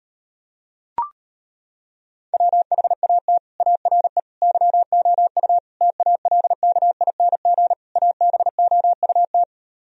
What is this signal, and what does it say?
Morse code, heard as a keyed tone, about 33 wpm, 700 Hz. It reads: WHAT ARE YOU TALKING ABOUT